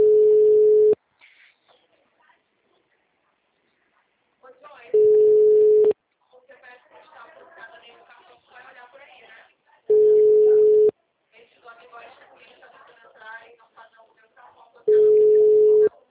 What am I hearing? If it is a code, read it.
Ringback tone (Europe, 425 Hz), with speech.